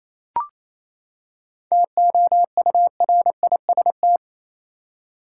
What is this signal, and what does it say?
Morse code, heard as a keyed tone, about 28 wpm, 700 Hz. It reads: TOURIST
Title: Seven Letter Words - 28wpm